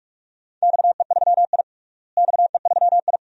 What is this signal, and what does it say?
Morse code, heard as a keyed tone, about 46 wpm, 700 Hz. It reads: XE3I XE3I
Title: DXCC-100 Call Signs - 4 Characters - Repeated 2x - 45wpm